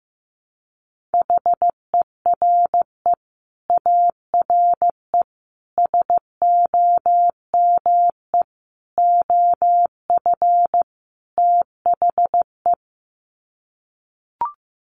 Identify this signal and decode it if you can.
Morse code, heard as a keyed tone, about 15 wpm, 700 Hz. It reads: HERE ARE SOME OF THE
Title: Sets of 5 Words Encore 15wpm